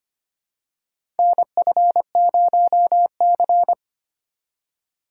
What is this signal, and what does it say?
Morse code, heard as a keyed tone, about 25 wpm, 700 Hz. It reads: NF0C
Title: USA Call Signs - 4 Characters - 3x Character Spacing - 25wpm